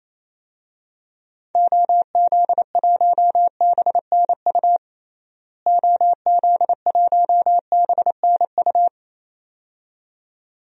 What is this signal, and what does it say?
Morse code, heard as a keyed tone, about 27 wpm, 700 Hz. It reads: OZ1BNU OZ1BNU
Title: DXCC-100 Call Signs - 6 Characters - Repeated 2x - 28wpm